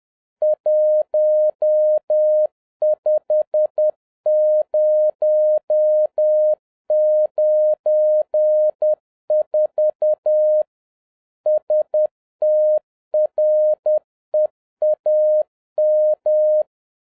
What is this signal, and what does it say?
Morse code, heard as a keyed tone, about 10 wpm, 610 Hz. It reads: 15094 STREAM
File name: 2011-10-mp3 / n01-CALLSIGN ASSIGNED 10wpm